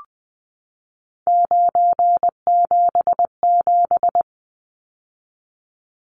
Morse code, heard as a keyed tone, about 20 wpm, 700 Hz. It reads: 977
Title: Numbers - 3-Digit 20wpm